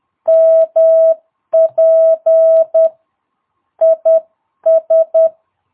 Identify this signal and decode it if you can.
Morse code, heard as a keyed tone, about 10 wpm, 650 Hz. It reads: MP IS
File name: jh8cbh / 2022cw